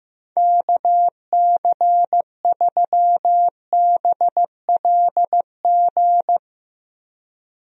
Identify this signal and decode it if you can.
Morse code, heard as a keyed tone, about 15 wpm, 700 Hz. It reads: KC3BLG
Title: USA Call Signs - 6 Characters - Repeated 3x - 15wpm